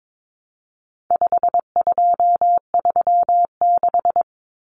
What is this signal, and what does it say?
Morse code, heard as a keyed tone, about 22 wpm, 700 Hz. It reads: 5236
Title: Numbers - 4-Digit 22wpm